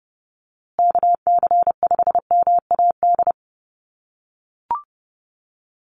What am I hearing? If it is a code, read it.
Morse code, heard as a keyed tone, about 30 wpm, 700 Hz. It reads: KC5MAD
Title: USA Call Signs Repeated 3x 30wpm